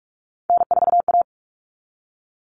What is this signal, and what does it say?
Morse code, heard as a keyed tone, about 44 wpm, 700 Hz. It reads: N4A
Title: Common Cut Numbers - 3-Digit 45wpm